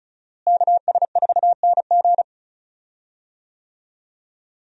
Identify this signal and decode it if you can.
Morse code, heard as a keyed tone, about 36 wpm, 700 Hz. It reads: KS4NG
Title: USA Call Signs - 5 Characters - Repeated 2x - 35wpm